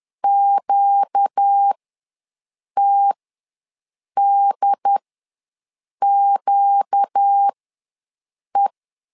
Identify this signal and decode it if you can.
Morse code, heard as a keyed tone, about 10 wpm, spaced as for 6 wpm, 800 Hz. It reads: QTDQE
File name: LEZIONI / 4 Q ws 10wpm 2306